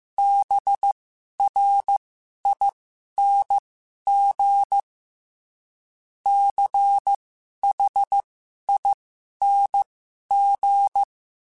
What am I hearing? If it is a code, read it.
Morse code, heard as a keyed tone, about 15 wpm, spaced as for 11 wpm, 800 Hz. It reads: BRING CHING
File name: Special INGs-mp3s / Special INGs-mp3s / Sngl Ing @10